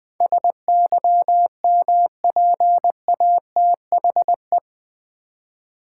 Morse code, heard as a keyed tone, about 20 wpm, 700 Hz. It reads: SYMPATHE
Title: Build A Word - 4 Letters At A Time - 11-Letter Words - 20wpm